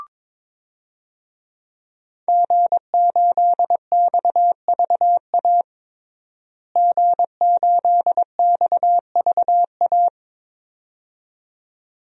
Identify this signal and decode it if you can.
Morse code, heard as a keyed tone, about 22 wpm, 700 Hz. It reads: G8XVA G8XVA
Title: DXCC-100 Call Signs - 5 Characters - Repeated 2x - 22wpm